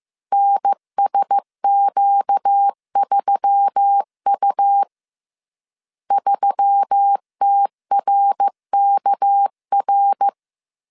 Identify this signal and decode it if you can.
Morse code, heard as a keyed tone, about 15 wpm, 800 Hz. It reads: NSQ3U 3TRKR
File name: CW / DET / 1 6 1500C S C 14wpm 2951